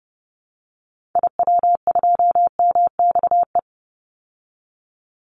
Morse code, heard as a keyed tone, about 30 wpm, 700 Hz. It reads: IW2MXE
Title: Call Signs Repeated 2x - 30wpm